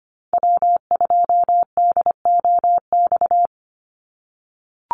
Morse code, heard as a keyed tone, about 25 wpm, 700 Hz. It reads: W2DOX
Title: USA Call Signs 25wpm